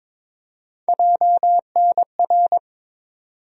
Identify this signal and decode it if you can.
Morse code, heard as a keyed tone, about 22 wpm, 700 Hz. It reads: JNR
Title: USA Call Sign Postfixes - 3 Letters 22wpm